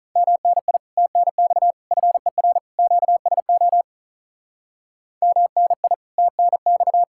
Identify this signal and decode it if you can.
Morse code, heard as a keyed tone, about 46 wpm, 700 Hz. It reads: MNI TNX FER QSO MNI TNX